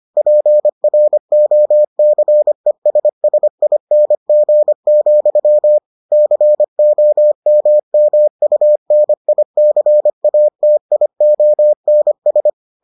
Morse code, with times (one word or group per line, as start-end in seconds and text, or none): 0.2-5.8 PROCESSING,
6.1-12.5 COMMUNICATIONS